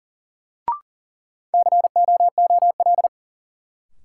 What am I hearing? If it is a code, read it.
Morse code, heard as a keyed tone, about 40 wpm, 700 Hz. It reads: COOL
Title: Sentences from Top 1200 Words - New Word Drill - 40wpm